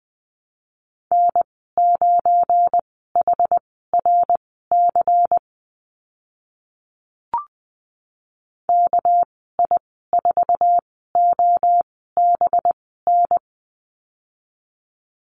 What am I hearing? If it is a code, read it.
Morse code, heard as a keyed tone, about 20 wpm, spaced as for 14 wpm, 700 Hz. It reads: N9HRC KI4OBN
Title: 219 Lesson 35g - Number 0 - Call Signs 20wpm